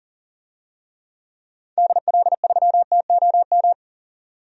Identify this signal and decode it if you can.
Morse code, heard as a keyed tone, about 40 wpm, 700 Hz. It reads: DL3TOM
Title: DXCC-100 Call Signs - 6 Characters - 40wpm